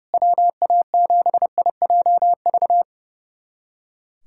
Morse code, heard as a keyed tone, about 32 wpm, 700 Hz. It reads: WA7IJV
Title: USA Callsign DE Callsign - 6 Characters - 30wpm